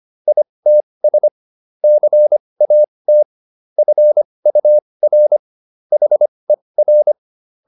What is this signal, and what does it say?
Morse code, heard as a keyed tone, about 25 wpm, spaced as for 20 wpm, 600 Hz. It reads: ITS CAT FUR HER